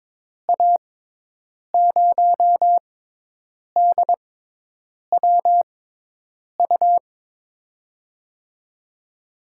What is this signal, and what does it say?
Morse code, heard as a keyed tone, about 22 wpm, spaced as for 8 wpm, 700 Hz. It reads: A0DWU
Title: USA Call Signs - 6 Characters - 6x Character Spacing - 22wpm